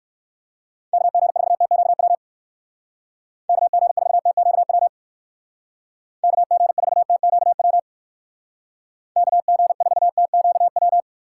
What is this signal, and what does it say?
Morse code, heard as a keyed tone, about 59 wpm, 700 Hz. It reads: KG4TQW KG4TQW KG4TQW KG4TQW